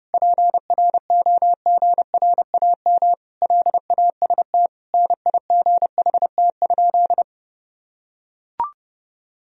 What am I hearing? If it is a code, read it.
Morse code, heard as a keyed tone, about 30 wpm, 700 Hz. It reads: PROGRAM LAST NIGHT?
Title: Sentences from Top 1000 Words - Review Part 4 - 30wpm